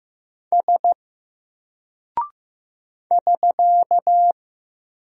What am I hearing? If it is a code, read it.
Morse code, heard as a keyed tone, about 15 wpm, 700 Hz. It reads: S <SK>
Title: Single Character (Letter:Number:Punctuation:Prosign) 15wpm